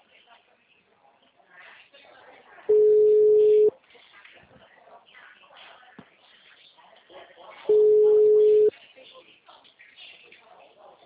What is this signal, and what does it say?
Ringback tone (Europe, 425 Hz), with speech.